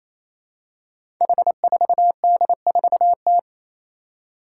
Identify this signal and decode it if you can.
Morse code, heard as a keyed tone, about 27 wpm, 700 Hz. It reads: H4D4T
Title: Five Challenging Letter-Number Combinations 28wpm